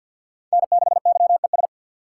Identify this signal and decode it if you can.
Morse code, heard as a keyed tone, about 48 wpm, 700 Hz. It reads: N6YES
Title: USA Call Signs - 5 Characters - 3x Character Spacing - 50wpm